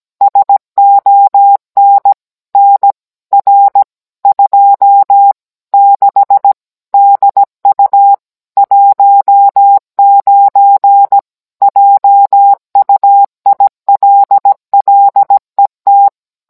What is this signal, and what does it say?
Morse code, heard as a keyed tone, about 17 wpm, 800 Hz. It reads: SON NR 26 DU 19 JUILLET